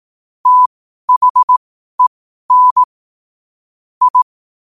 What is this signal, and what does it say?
Morse code, heard as a keyed tone, about 18 wpm, spaced as for 12 wpm, 1000 Hz. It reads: THEN I
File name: qotdfiles / 12wpm